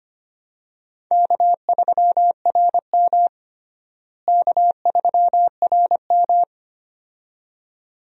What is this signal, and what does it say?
Morse code, heard as a keyed tone, about 25 wpm, 700 Hz. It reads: K3RM K3RM
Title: USA Call Signs - 4 Characters - Repeated 2x - 25wpm